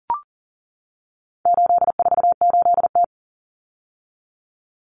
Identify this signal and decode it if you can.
Morse code, heard as a keyed tone, about 40 wpm, 700 Hz. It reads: 848T
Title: Common Cut Numbers - 4-Digit 40wpm